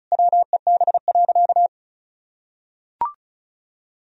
Morse code, heard as a keyed tone, about 34 wpm, 700 Hz. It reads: WEB.
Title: Sentences from Top 1000 Words - Review Part 4 - 35wpm